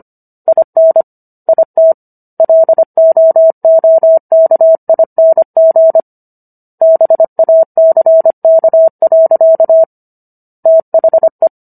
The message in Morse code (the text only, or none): IN IT LOOKING BACK. THE